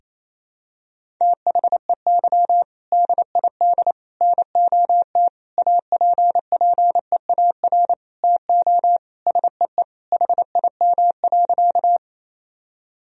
Morse code, heard as a keyed tone, about 27 wpm, 700 Hz. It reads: THEY DID NOT APPEAR TO SEE HIM.
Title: Sentences from Top 1000 Words - Review Part 2 - 28wpm